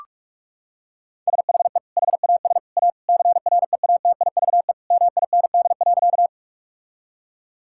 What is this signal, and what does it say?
Morse code, heard as a keyed tone, about 46 wpm, 700 Hz. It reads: SHE HAS A CREATIVE MIND.
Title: Sentences from Top 1000 Words - Review Part 5 - 45wpm